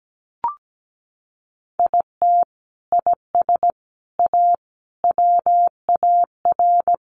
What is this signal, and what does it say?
Morse code, heard as a keyed tone, about 17 wpm, 700 Hz. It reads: IT IS A WAR
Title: Sentences from Top 1500 Words 17wpm